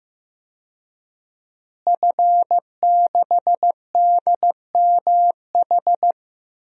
Morse code, heard as a keyed tone, about 16 wpm, 700 Hz. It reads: F6DMH
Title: DXCC-100 Call Signs - 5 Characters - Repeated 3x - 15wpm